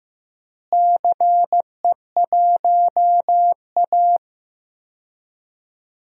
Morse code, heard as a keyed tone, about 15 wpm, 700 Hz. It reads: CE1A
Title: DXCC-100 Call Signs - 4 Characters - Repeated 2x - 15wpm